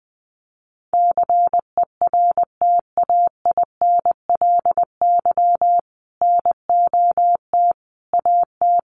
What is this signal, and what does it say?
Morse code, heard as a keyed tone, about 20 wpm, 700 Hz. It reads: CERTAINLY NOT AT
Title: Sentences from Top 900 Words 20wpm